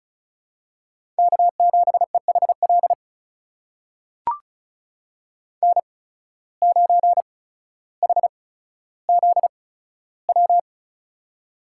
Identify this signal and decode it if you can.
Morse code, heard as a keyed tone, about 36 wpm, spaced as for 15 wpm, 700 Hz. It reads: K7EHL N9HZW
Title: USA Call Signs - 5 Characters - 8x Character Spacing - 35wpm